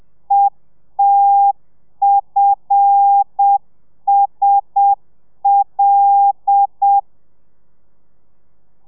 Morse code, heard as a keyed tone, about 7 wpm, 800 Hz. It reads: ETFSL
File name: telegrafia / i2tfj / Lez5 30